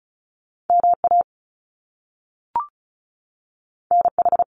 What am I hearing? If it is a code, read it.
Morse code, heard as a keyed tone, about 36 wpm, 700 Hz. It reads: MA NH
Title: US States 35wpm